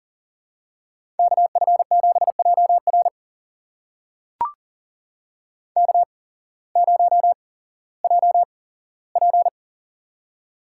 Morse code, heard as a keyed tone, about 39 wpm, spaced as for 18 wpm, 700 Hz. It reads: KF7JR K0JP
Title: USA Call Signs - 8x Character Spacing - 40wpm